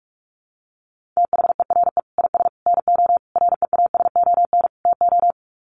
Morse code, heard as a keyed tone, about 44 wpm, 700 Hz. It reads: THERE IS NO REASON TO
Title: Sets of 5 Words Encore 45wpm